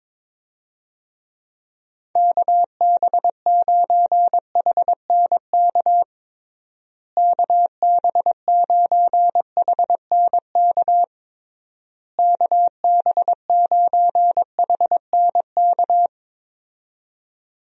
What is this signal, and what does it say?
Morse code, heard as a keyed tone, about 22 wpm, 700 Hz. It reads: KB9HNK KB9HNK KB9HNK
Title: USA Call Signs - 6 Characters - Repeated 3x - 22wpm